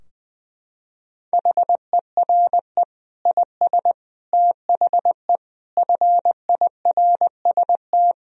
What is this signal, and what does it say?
Morse code, heard as a keyed tone, about 21 wpm, 700 Hz. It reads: HERE IS THE FIRST
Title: Sentences from Top 1000 Words - Review Part 2 - 20wpm